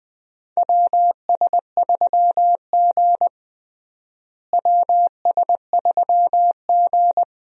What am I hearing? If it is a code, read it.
Morse code, heard as a keyed tone, about 20 wpm, 700 Hz. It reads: WS3G WS3G
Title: USA Call Signs - 4 Characters - Repeated 3x - 20wpm